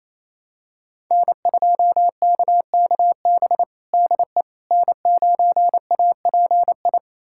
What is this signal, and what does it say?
Morse code, heard as a keyed tone, about 28 wpm, 700 Hz. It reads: N2KKB DE N9API
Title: USA Callsign DE Callsign - 5 Characters - 28wpm